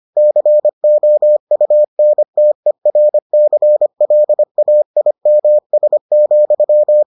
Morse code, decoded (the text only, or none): COUNTERCLAIMS,